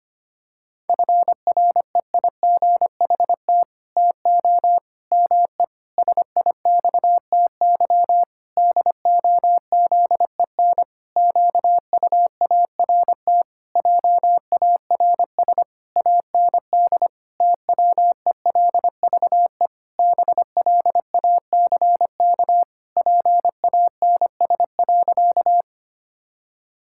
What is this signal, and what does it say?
Morse code, heard as a keyed tone, about 25 wpm, 700 Hz. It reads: FREIGHT TO ME SIXTY DOZEN QUART JARS AND TWELVE BLACK PANS.
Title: Pangram Challenge v2 - 25wpm